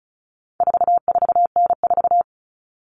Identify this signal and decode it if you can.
Morse code, heard as a keyed tone, about 34 wpm, 700 Hz. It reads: 44N4
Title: All Cut Numbers - 4-Digit 35wpm